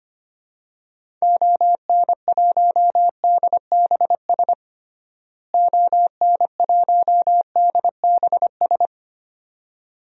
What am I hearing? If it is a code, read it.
Morse code, heard as a keyed tone, about 25 wpm, 700 Hz. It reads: ON1DBS ON1DBS
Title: DXCC-100 Call Signs - 6 Characters - Repeated 2x - 25wpm